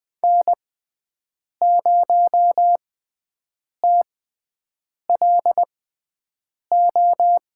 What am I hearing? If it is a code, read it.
Morse code, heard as a keyed tone, about 20 wpm, spaced as for 7 wpm, 700 Hz. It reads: N0TLO
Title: USA Call Signs - 6x Character Spacing - 20wpm